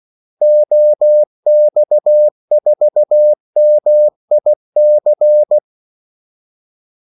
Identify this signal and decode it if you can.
Morse code, heard as a keyed tone, about 16 wpm, 600 Hz. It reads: OX4MIC
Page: Kaldesignaler 16 wpm